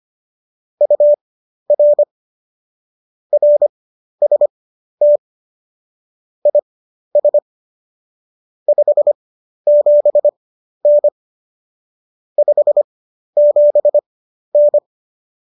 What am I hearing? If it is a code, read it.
Morse code, heard as a keyed tone, about 25 wpm, spaced as for 12 wpm, 590 Hz. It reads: UR RST IS 57N 57N
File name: qso012 12